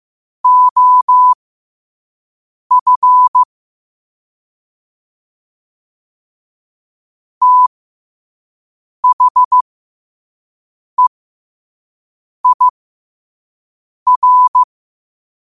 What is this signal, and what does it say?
Morse code, heard as a keyed tone, about 15 wpm, spaced as for 5 wpm, 1000 Hz. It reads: OF THEIR
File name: qotdfiles / 5wpm